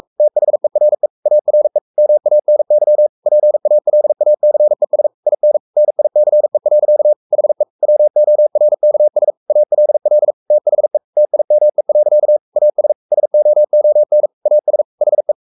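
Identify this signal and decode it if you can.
Morse code, heard as a keyed tone, about 44 wpm, 600 Hz. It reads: THERE ARE MANY PALACES IN NICE. HE WORKS ALL THE TIME. AS SOON AS HE
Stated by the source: Morse code at 43 wpm sending Ken Follett's novel Never in English, from CW med Gnister.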